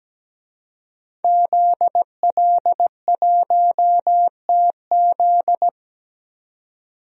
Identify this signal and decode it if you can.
Morse code, heard as a keyed tone, about 17 wpm, 700 Hz. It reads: ZL1TZ
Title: DXCC-100 Call Signs - 5 Characters - 17wpm